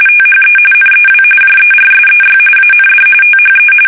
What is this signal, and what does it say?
RTTY, about 45.45 baud. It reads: THE NEW JERSEY AND THE
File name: fsk850 / battleshipnj